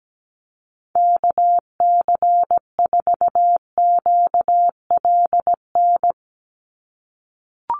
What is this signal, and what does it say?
Morse code, heard as a keyed tone, about 17 wpm, 700 Hz. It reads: KC4QLN
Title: USA Call Signs 17wpm